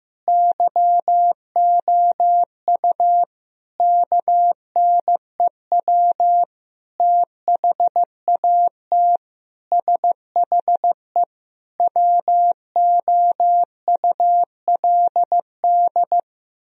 Morse code, heard as a keyed tone, about 15 wpm, 700 Hz. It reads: YOU KNEW THAT SHE WOULD
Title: Sentences from Top 1000 Words - Review Part 2 - 15wpm